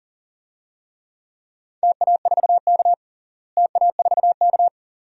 Morse code, heard as a keyed tone, about 40 wpm, 700 Hz. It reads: TA4K TA4K
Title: DXCC-100 Call Signs - 4 Characters - Repeated 2x - 40wpm